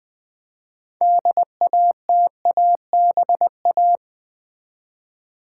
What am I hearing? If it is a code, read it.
Morse code, heard as a keyed tone, about 20 wpm, 700 Hz. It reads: DATABA
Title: Build A Word - 2 Letters At A Time - 8-Letter Words - 20wpm